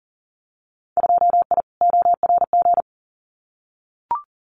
Morse code, heard as a keyed tone, about 38 wpm, 700 Hz. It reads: 2I ORG
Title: Winter Field Day Exchanges Encore 40wpm